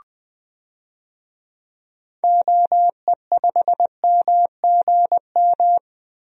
Morse code, heard as a keyed tone, about 21 wpm, 700 Hz. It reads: OE5MGM
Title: DXCC-100 Call Signs - 6 Characters - 20wpm